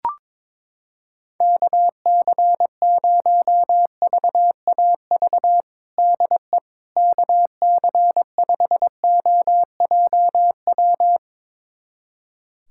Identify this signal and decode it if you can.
Morse code, heard as a keyed tone, about 22 wpm, 700 Hz. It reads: KC0VAV DE KC5OJW
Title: USA Callsign DE Callsign 22wpm